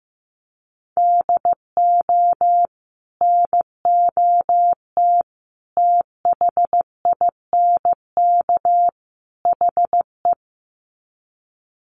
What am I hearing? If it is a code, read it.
Morse code, heard as a keyed tone, about 15 wpm, 700 Hz. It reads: DO NOT THINK HE
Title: Sets of 4 Words 15wpm